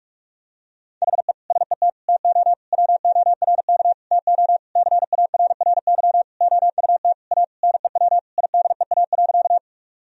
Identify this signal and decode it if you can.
Morse code, heard as a keyed tone, about 46 wpm, 700 Hz. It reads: HE SET TO WORK TO CARRY OUT A NEW IDEA.
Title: Sentences from Top 1000 Words - Review Part 1 - 45wpm